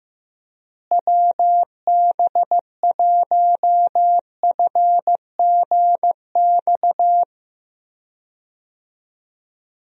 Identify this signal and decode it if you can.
Morse code, heard as a keyed tone, about 15 wpm, 700 Hz. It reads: WB1FGX
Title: USA Call Signs - 6 Characters - Repeated 3x - 15wpm